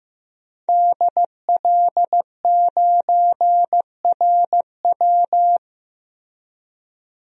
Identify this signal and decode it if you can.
Morse code, heard as a keyed tone, about 15 wpm, 700 Hz. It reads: DL9RW
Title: DXCC-100 Call Signs - 5 Characters - 15wpm